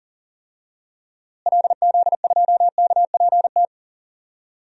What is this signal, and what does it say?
Morse code, heard as a keyed tone, about 40 wpm, 700 Hz. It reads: LZ2KPT
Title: DXCC-100 Call Signs - 6 Characters - Repeated 3x - 40wpm